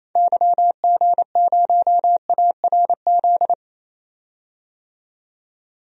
Morse code, heard as a keyed tone, about 27 wpm, 700 Hz. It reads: YG0ARZ
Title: DXCC-100 Call Signs - 6 Characters - 28wpm